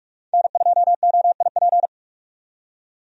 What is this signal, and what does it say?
Morse code, heard as a keyed tone, about 44 wpm, 700 Hz. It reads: N2OIP
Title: USA Call Signs - 5 Characters - 45wpm